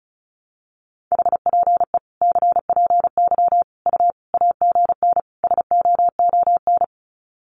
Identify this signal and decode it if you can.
Morse code, heard as a keyed tone, about 34 wpm, 700 Hz. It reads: HPE CPY U AGN SOON
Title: Common QSO Phrases 35wpm